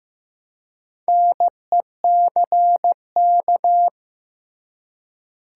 Morse code, heard as a keyed tone, about 15 wpm, 700 Hz. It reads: NECK
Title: Four Letter Words - 15wpm